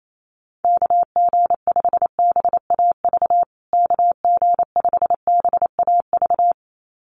Morse code, heard as a keyed tone, about 28 wpm, 700 Hz. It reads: KG5BAV KG5BAV
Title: USA Call Signs Repeated 2x 28wpm